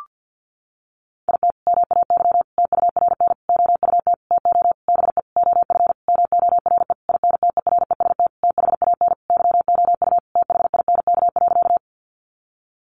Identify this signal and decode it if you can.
Morse code, heard as a keyed tone, about 50 wpm, 700 Hz. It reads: IT MAY TURN OUT TO BE OF MORE INTEREST THAN YOU THINK.
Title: Sentences from Top 400 Words 50wpm